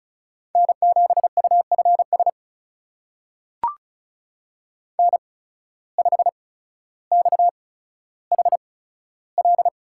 Morse code, heard as a keyed tone, about 36 wpm, spaced as for 15 wpm, 700 Hz. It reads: N7UFS N5XHL
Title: USA Call Signs - 5 Characters - 8x Character Spacing - 35wpm